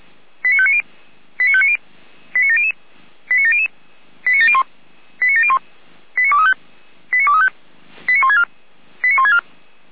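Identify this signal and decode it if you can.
Five-tone selective call (ZVEI), heard as quick runs of five tones, tones of about 70 ms. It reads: 89590 89590 89700 89700 89701 89701 89235 89235 89164 89164
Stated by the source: ZVEI five-tone call 89304 received via SDR.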